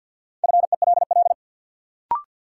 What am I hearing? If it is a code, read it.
Morse code, heard as a keyed tone, about 52 wpm, 700 Hz. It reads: FELL
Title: Sentences from Top 1600 Words - New Word Drill Alt - 50wpm